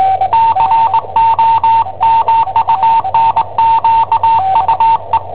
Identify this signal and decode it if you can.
Morse code, heard as a keyed tone, about 20 wpm, 910 Hz, with radio noise. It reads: COMUNQUE
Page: Cassetta esercizi preparata da I7OHP per IW5ELP